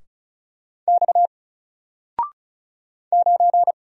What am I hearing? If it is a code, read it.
Morse code, heard as a keyed tone, about 36 wpm, 700 Hz. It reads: X 9
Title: Single Letter-Number 35wpm